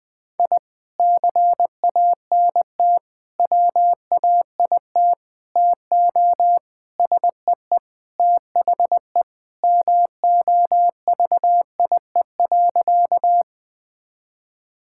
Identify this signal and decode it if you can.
Morse code, heard as a keyed tone, about 21 wpm, 700 Hz. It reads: I CANT WAIT TO SEE THE MOVIE.
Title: Sentences with 3-Word Phrases - 20wpm